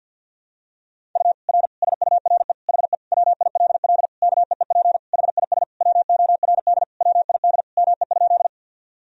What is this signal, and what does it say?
Morse code, heard as a keyed tone, about 51 wpm, 700 Hz. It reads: U R SURE HE WILL KEEP HIS WORD WID ME?
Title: Sentences from Top 300 Words and Abbreviations 50wpm